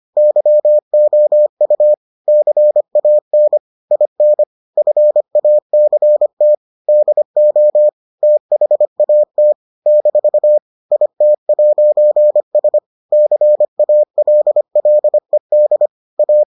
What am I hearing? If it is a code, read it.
Morse code, heard as a keyed tone, about 25 wpm, 600 Hz. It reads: YOU CAN IN FACT DO THAT - IT'S CALLED A